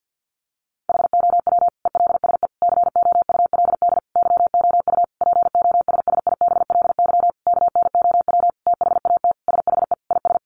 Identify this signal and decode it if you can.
Morse code, heard as a keyed tone, about 50 wpm, 700 Hz. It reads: HOW ELSE COULD YOU POSSIBLY KNOW THAT SHE IS
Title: Sentences from Top 1300 Words 50wpm